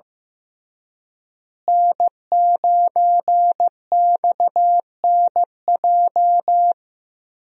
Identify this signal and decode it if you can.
Morse code, heard as a keyed tone, about 14 wpm, 700 Hz. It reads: N9XNJ
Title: USA Call Signs - 5 Characters - Repeated 3x - 15wpm